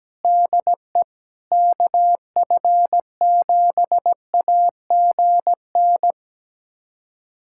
Morse code, heard as a keyed tone, about 17 wpm, 700 Hz. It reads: DE KF7AGN
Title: USA Callsign DE Callsign - 6 Characters - 17wpm